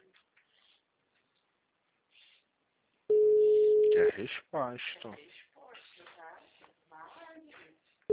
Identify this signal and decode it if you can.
Ringback tone (Europe, 425 Hz), with speech.